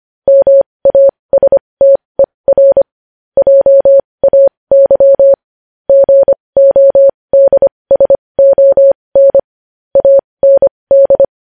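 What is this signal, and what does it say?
Morse code, heard as a keyed tone, about 25 wpm, spaced as for 20 wpm, 550 Hz. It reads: MASTER JAY GODSON AND
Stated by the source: Morse code news headlines sent at 20 wpm.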